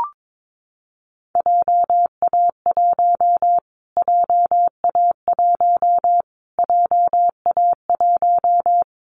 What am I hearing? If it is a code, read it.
Morse code, heard as a keyed tone, about 22 wpm, 700 Hz. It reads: JA1 JA1 JA1
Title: Callsign Prefixes for Top 25 DXCC - Repeated 3x - 22wpm